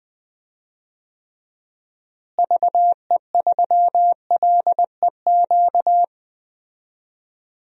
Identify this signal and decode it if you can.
Morse code, heard as a keyed tone, about 20 wpm, 700 Hz. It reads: VE3LEQ
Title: DXCC-100 Call Signs - 6 Characters - Repeated 3x - 20wpm